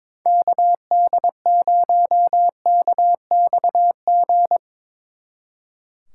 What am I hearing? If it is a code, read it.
Morse code, heard as a keyed tone, about 24 wpm, 700 Hz. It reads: KD0KXG
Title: Speed Racing - USA Call Signs 22wpm